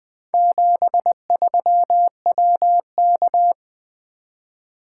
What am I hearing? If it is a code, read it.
Morse code, heard as a keyed tone, about 20 wpm, 700 Hz. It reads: 73WK
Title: Four Letter-Number Combinations - 20wpm